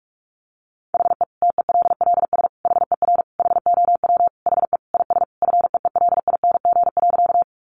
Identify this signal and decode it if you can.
Morse code, heard as a keyed tone, about 44 wpm, 700 Hz. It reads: HE TELLS HER HOW HE IS FEELING.
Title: Additional Easy Phrases 45wpm